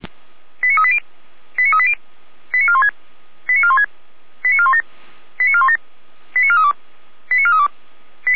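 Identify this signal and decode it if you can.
Five-tone selective call (ZVEI), heard as quick runs of five tones, tones of about 70 ms. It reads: 89380 89380 89416 89416 89417 89417 89432 89432
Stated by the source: ZVEI five-tone call received by SDR for feuerwehr-orlamuende, a fire brigade at Orlamünde.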